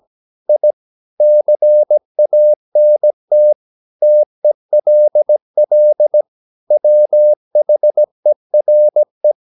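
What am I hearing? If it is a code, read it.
Morse code, heard as a keyed tone, about 17 wpm, 600 Hz. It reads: I CANT TELL WHERE
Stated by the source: 17 wpm Morse practice sending the English text of a novel.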